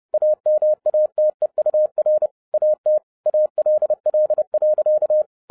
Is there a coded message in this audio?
Yes: Morse code from 0.1 to 5.2 s, heard as a keyed tone, about 30 wpm, 610 Hz. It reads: AMATEUR AT ALL.